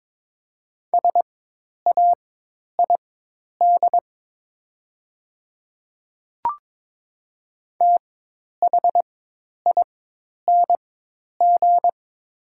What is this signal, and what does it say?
Morse code, heard as a keyed tone, about 22 wpm, spaced as for 10 wpm, 700 Hz. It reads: SAID THING
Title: 4x Character Spacing - Top 100 Words - 22wpm